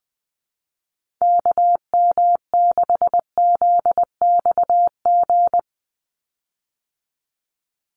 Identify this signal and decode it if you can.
Morse code, heard as a keyed tone, about 20 wpm, 700 Hz. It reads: KM6ZXG
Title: USA Call Signs 20wpm